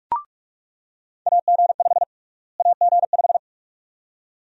Morse code, heard as a keyed tone, about 42 wpm, 700 Hz. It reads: AG5 AG5
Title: USA Call Sign Prefixes - Repeated 2x - 45wpm